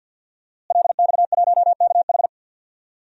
Morse code, heard as a keyed tone, about 48 wpm, 700 Hz. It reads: LX1KH
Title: DXCC-100 Call Signs - 5 Characters - Repeated 3x - 50wpm